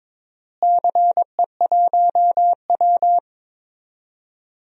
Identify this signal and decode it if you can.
Morse code, heard as a keyed tone, about 22 wpm, 700 Hz. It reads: CE1W
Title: DXCC-100 Call Signs - 4 Characters - Repeated 2x - 22wpm